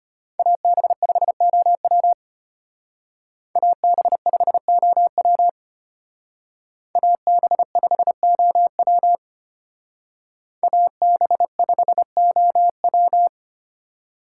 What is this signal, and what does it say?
Morse code, heard as a keyed tone, about 32 wpm, 700 Hz. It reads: AB5OW AB5OW AB5OW AB5OW